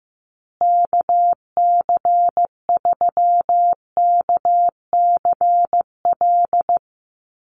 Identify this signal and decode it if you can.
Morse code, heard as a keyed tone, about 15 wpm, 700 Hz. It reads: KC3KCL
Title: USA Call Signs Repeated 3x 15wpm